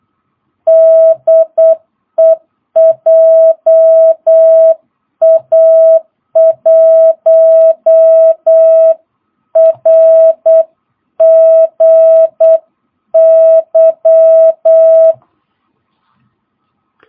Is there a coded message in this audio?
Yes: Morse code from 0.7 to 15.1 s, heard as a keyed tone, about 8 wpm, 650 Hz. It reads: DE JA1RGY